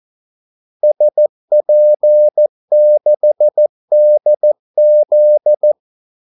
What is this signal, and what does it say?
Morse code, heard as a keyed tone, about 14 wpm, 600 Hz. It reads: SP6DZ
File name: Callsigns-14wpm